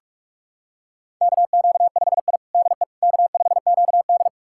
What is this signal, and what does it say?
Morse code, heard as a keyed tone, about 41 wpm, 700 Hz. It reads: KQ5I DE K5QD
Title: USA Callsign DE Callsign - 4 Characters - 45wpm